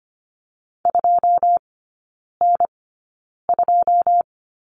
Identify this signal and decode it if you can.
Morse code, heard as a keyed tone, about 25 wpm, spaced as for 9 wpm, 700 Hz. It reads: 2N2
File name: course / 25:10wpm / 093 Lesson 16e - Letter c - Three Characters 25wpm